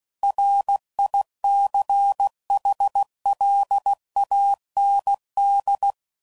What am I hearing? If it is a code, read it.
Morse code, heard as a keyed tone, about 16 wpm, 800 Hz. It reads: RICHLAND
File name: CW Code Groups @15WPM / fromSM's / Cities